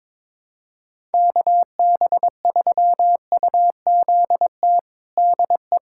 Morse code, heard as a keyed tone, about 22 wpm, 700 Hz. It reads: KB3UZT DE
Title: USA Callsign DE Callsign 22wpm